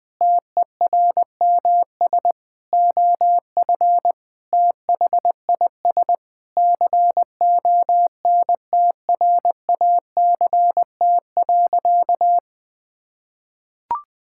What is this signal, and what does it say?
Morse code, heard as a keyed tone, about 21 wpm, 700 Hz. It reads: TERMS OF THIS CONTRACT.
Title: Sentences with 9-Word Phrases - 20wpm